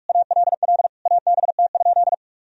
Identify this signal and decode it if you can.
Morse code, heard as a keyed tone, about 46 wpm, 700 Hz. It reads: ALL ABT?
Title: Sentences from Top 100 Words and Abbreviations 45wpm